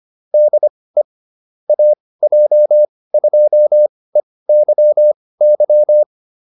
Morse code, heard as a keyed tone, about 25 wpm, spaced as for 18 wpm, 600 Hz. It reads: DE AJ2EYY